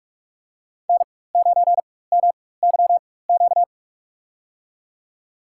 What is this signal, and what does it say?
Morse code, heard as a keyed tone, about 42 wpm, spaced as for 21 wpm, 700 Hz. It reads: N9MYQ
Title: USA Call Signs - 4x Character Spacing - 45wpm